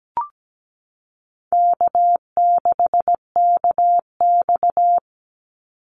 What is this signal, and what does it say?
Morse code, heard as a keyed tone, about 17 wpm, 700 Hz. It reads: K6KX
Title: Call Signs Repeated 2x - 17wpm